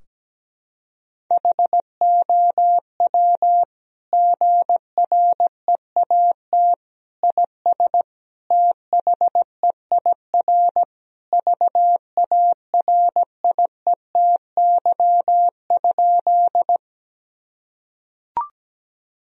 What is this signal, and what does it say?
Morse code, heard as a keyed tone, about 17 wpm, 700 Hz. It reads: HOW GREAT IS THEIR VARIETY?
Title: Sentences from Top 1000 Words - Review Part 4 - 17wpm